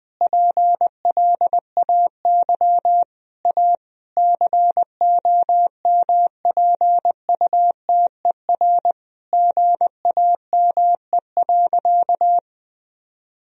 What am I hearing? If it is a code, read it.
Morse code, heard as a keyed tone, about 20 wpm, 700 Hz. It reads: PLAY A COMPUTER GAME.
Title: Sentences from Top 1000 Words - Review Part 3 - 20wpm